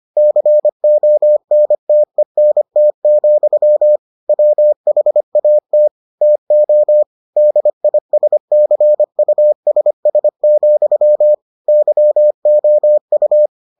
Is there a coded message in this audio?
Yes: Morse code from 0.2 to 13.5 s, heard as a keyed tone, about 25 wpm, 600 Hz. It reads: CONTENT, WHAT TO DISCUSS, YOU